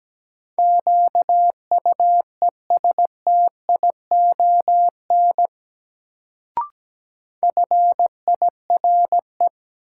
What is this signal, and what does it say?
Morse code, heard as a keyed tone, about 17 wpm, 700 Hz. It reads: QUESTION FIRE
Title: Sentences from Top 400 Words - New Word Drill - 17wpm